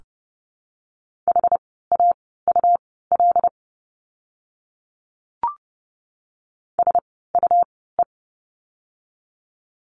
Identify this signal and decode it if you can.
Morse code, heard as a keyed tone, about 32 wpm, spaced as for 17 wpm, 700 Hz. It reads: HAUL SUE
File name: Lesson 17f - Letter u - Words 30wpm